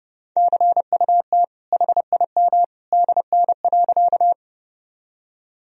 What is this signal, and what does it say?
Morse code, heard as a keyed tone, about 30 wpm, 700 Hz. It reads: CUT HIM DN.
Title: Sentences from Top 300 Words and Abbreviations 30wpm